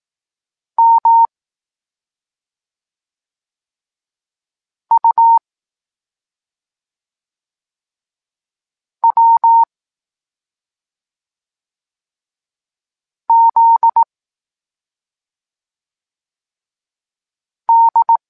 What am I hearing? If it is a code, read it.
Morse code, heard as a keyed tone, about 18 wpm, 920 Hz. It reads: M U W Z D
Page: Tekst 23 – Doskonalenie odbioru poznanych liter